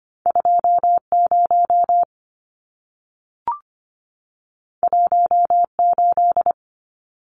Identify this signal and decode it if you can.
Morse code, heard as a keyed tone, about 25 wpm, 700 Hz. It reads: 20 18
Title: Numbers - 2-Digit 25wpm